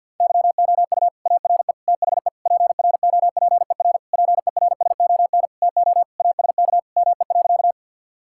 Morse code, heard as a keyed tone, about 50 wpm, 700 Hz. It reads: YOU ARE THE PROPER PERSON TO ASK ME.
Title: Sentences from Top 1000 Words - Review Part 5 - 50wpm